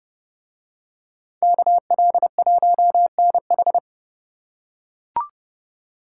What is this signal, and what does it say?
Morse code, heard as a keyed tone, about 29 wpm, 700 Hz. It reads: KL1NH